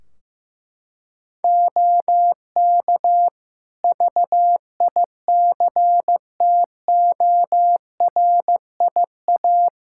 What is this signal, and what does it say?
Morse code, heard as a keyed tone, about 15 wpm, 700 Hz. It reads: OK VICTORIA
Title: Practice QSO - Exchange 3 - More Info - 15wpm